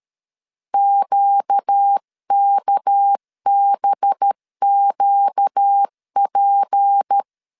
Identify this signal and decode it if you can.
Morse code, heard as a keyed tone, about 13 wpm, 800 Hz. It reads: QKBQP
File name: CW / DET / 1 6 1500C S C 12wpm 3533